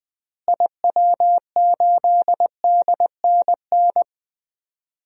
Morse code, heard as a keyed tone, about 20 wpm, 700 Hz. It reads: IW8DNN
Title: DXCC-100 Call Signs - 6 Characters - Repeated 3x - 20wpm